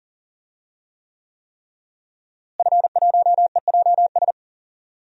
Morse code, heard as a keyed tone, about 40 wpm, 700 Hz. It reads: F1EJS
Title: DXCC-100 Call Signs - 5 Characters - Repeated 3x - 40wpm